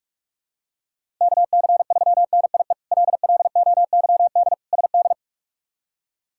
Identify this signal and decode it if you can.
Morse code, heard as a keyed tone, about 47 wpm, 700 Hz. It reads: KC3NIE LLOYD SD
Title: K1USN SST QSO Practice - 45wpm